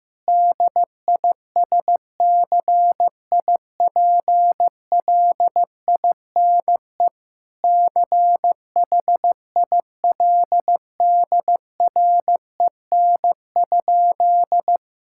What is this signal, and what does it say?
Morse code, heard as a keyed tone, about 15 wpm, 700 Hz. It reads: DISCIPLINE CHILDREN?
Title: Sentences from Top 1000 Words - Review Part 6 - 15wpm